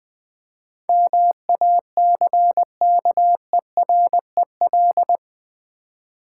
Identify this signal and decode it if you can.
Morse code, heard as a keyed tone, about 21 wpm, 700 Hz. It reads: MACKEREL
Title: Easy Words with Hints - 20wpm